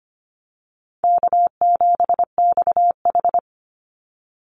Morse code, heard as a keyed tone, about 25 wpm, 700 Hz. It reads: K7XH
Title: Call Signs Repeated 2x - 25wpm